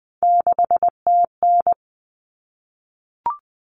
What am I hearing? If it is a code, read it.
Morse code, heard as a keyed tone, about 20 wpm, 700 Hz. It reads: 6TN
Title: Common Cut Numbers - 3-Digit 20wpm